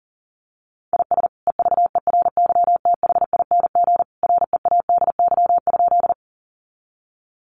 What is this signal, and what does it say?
Morse code, heard as a keyed tone, about 40 wpm, 700 Hz. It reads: IS EVERYTHING READY?
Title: Sentences from Top 700 Words 40wpm